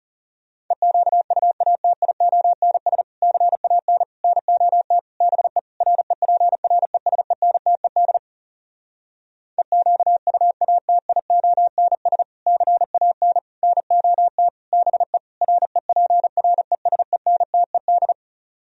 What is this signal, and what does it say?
Morse code, heard as a keyed tone, about 40 wpm, 700 Hz. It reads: EQUATIONS CAN NOT BE REPRESENTED EQUATIONS CAN NOT BE REPRESENTED